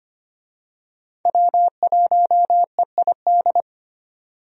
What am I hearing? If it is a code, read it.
Morse code, heard as a keyed tone, about 25 wpm, 700 Hz. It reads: W1EID
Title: USA Call Signs - 5 Characters - 25wpm